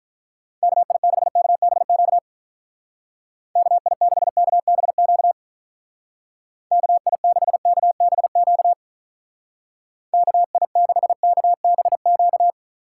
Morse code, heard as a keyed tone, about 44 wpm, 700 Hz. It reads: KI6KBQ KI6KBQ KI6KBQ KI6KBQ